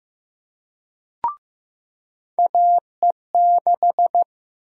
Morse code, heard as a keyed tone, about 15 wpm, 700 Hz. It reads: AE6
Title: USA Call Sign Prefixes - 15wpm